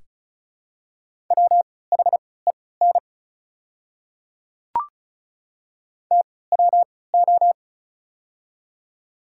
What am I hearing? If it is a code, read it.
Morse code, heard as a keyed tone, about 36 wpm, spaced as for 20 wpm, 690 Hz. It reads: WHEN TWO
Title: 3x Character Spacing - Top 100 Words - 35wpm